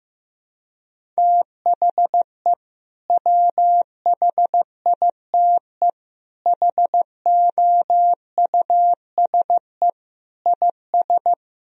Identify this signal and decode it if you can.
Morse code, heard as a keyed tone, about 15 wpm, 700 Hz. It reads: THE WHITE HOUSE IS
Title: Sentences from Top 1000 Words - Review Part 1 - 15wpm